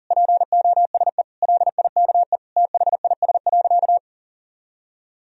Morse code, heard as a keyed tone, about 37 wpm, 700 Hz. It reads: POSE LIKE THIS.
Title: Sentences from Top 1000 Words - Review Part 2 - 40wpm